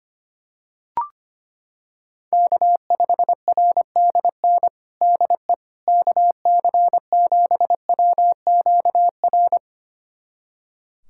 Morse code, heard as a keyed tone, about 25 wpm, 700 Hz. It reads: K5RDN DE KC7WQR
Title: USA Callsign DE Callsign 25wpm